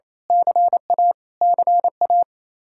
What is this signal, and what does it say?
Morse code, heard as a keyed tone, about 28 wpm, 700 Hz. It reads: CA CA
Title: Speed-Racing Common QSO Phrases - 28wpm